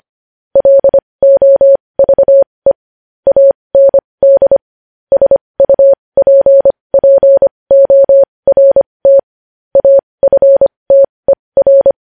Morse code, heard as a keyed tone, about 25 wpm, spaced as for 20 wpm, 550 Hz. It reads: LOVE AND SUPPORT AFTER
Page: news-headlines-in-morse-code-at-20-wpm